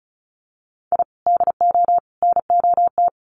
Morse code, heard as a keyed tone, about 34 wpm, 700 Hz. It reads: I DO NOT
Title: Sets of 3 Words 35wpm